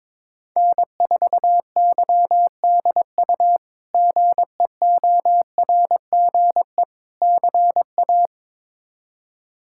Morse code, heard as a keyed tone, about 22 wpm, 700 Hz. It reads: N4YDU GEORGE CA
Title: K1USN SST QSO Practice - 22wpm